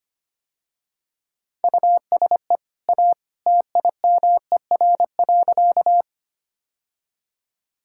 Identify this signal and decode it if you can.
Morse code, heard as a keyed tone, about 25 wpm, 700 Hz. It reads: USE A TIMER.
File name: Easy Sentences with Hints - 25wpm